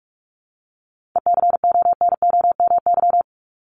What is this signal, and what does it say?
Morse code, heard as a keyed tone, about 44 wpm, 700 Hz. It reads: ECONOMY
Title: Top 501-750 Words 45wpm